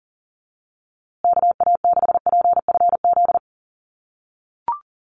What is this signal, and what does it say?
Morse code, heard as a keyed tone, about 40 wpm, 700 Hz. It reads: KA6PFZ
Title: USA Call Signs Repeated 2x 40wpm